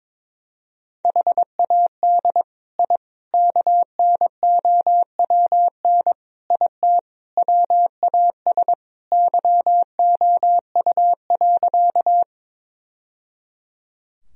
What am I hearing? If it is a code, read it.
Morse code, heard as a keyed tone, about 22 wpm, 700 Hz. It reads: HAD I KNOWN IT WAS YOU.
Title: Sentences from Top 1000 Words - Review Part 3 - 22wpm